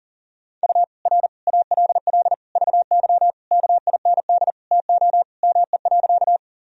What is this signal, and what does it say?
Morse code, heard as a keyed tone, about 37 wpm, 700 Hz. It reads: U R ALL VY KIND TO ME.
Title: Sentences from Top 200 Words and Abbreviations 40wpm